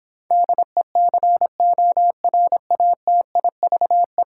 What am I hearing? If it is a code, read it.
Morse code, heard as a keyed tone, about 26 wpm, 700 Hz. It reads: DECORATIVE